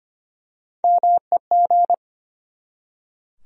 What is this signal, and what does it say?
Morse code, heard as a keyed tone, about 25 wpm, 700 Hz. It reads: MEG
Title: USA Common Names - Max 3 Letters - 25wpm